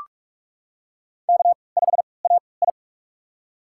Morse code, heard as a keyed tone, about 42 wpm, spaced as for 25 wpm, 690 Hz. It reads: K5AI
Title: USA Call Signs - 4 Characters - 3x Character Spacing - 45wpm